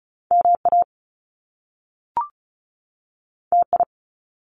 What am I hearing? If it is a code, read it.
Morse code, heard as a keyed tone, about 36 wpm, 700 Hz. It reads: MA TI
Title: Top Two Letter Combinations in Words 35wpm